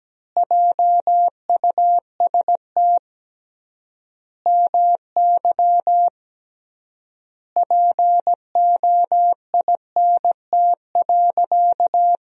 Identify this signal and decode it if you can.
Morse code, heard as a keyed tone, about 17 wpm, 700 Hz. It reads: JUST MY POINT.
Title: 3x Word-Spacing - Sentences from Top 200 Words - 17wpm